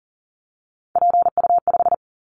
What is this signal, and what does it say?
Morse code, heard as a keyed tone, about 40 wpm, 700 Hz. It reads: PU5
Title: Callsign Prefixes for Top 50 DXCC - 40wpm